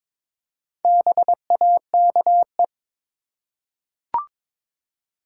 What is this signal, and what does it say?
Morse code, heard as a keyed tone, about 22 wpm, 700 Hz. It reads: BAKE
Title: Four Letter Words - 22wpm